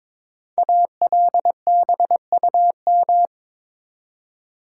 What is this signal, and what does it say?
Morse code, heard as a keyed tone, about 22 wpm, 700 Hz. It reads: ALBUM
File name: 5 Letter Words - First 3 Letters Given - 22wpm